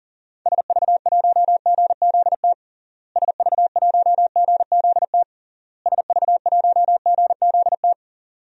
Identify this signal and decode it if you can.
Morse code, heard as a keyed tone, about 40 wpm, 700 Hz. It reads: SV1GZT SV1GZT SV1GZT
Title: DXCC-100 Call Signs - 6 Characters - Repeated 3x - 40wpm